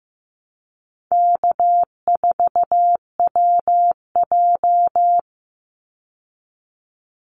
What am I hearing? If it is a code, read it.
Morse code, heard as a keyed tone, about 15 wpm, 700 Hz. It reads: K4WJ
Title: Easy Call Signs Repeated 3x - 15wpm